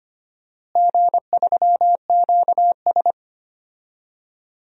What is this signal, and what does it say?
Morse code, heard as a keyed tone, about 25 wpm, 700 Hz. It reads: G3QS
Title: DXCC-100 Call Signs - 4 Characters - Repeated 2x - 25wpm